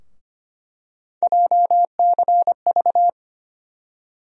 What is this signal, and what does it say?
Morse code, heard as a keyed tone, about 25 wpm, 700 Hz. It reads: JCV
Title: USA Call Sign Postfixes - 3 Letters 25wpm